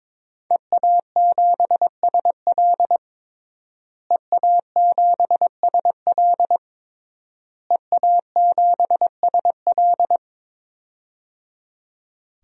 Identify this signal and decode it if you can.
Morse code, heard as a keyed tone, about 22 wpm, 700 Hz. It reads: EA7SL EA7SL EA7SL
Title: DXCC-100 Call Signs - 5 Characters - Repeated 3x - 22wpm